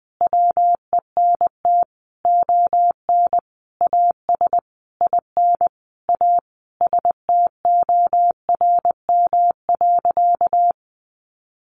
Morse code, heard as a keyed tone, about 19 wpm, 700 Hz. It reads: WENT ON AS IN A STORM.
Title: Sentences from Top 1500 Words 20wpm